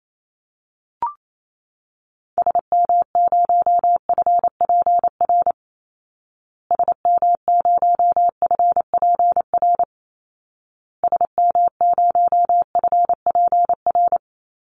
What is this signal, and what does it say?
Morse code, heard as a keyed tone, about 28 wpm, 700 Hz. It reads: SM0FPR SM0FPR SM0FPR
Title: Call Signs Repeated 2x - 28wpm